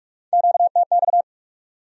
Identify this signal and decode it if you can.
Morse code, heard as a keyed tone, about 62 wpm, spaced as for 54 wpm, 700 Hz. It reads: QTX
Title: Speed-Racing Q-Codes - 45wpm